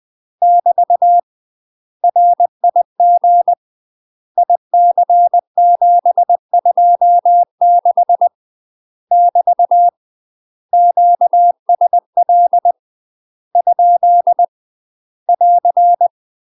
Morse code, heard as a keyed tone, about 20 wpm, 700 Hz. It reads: <BT> RIG IC726 <BT> QSL ? <AR>
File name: UK MASTER QSOs TRAINING / 20WPM / UK MASTER QSOs 0025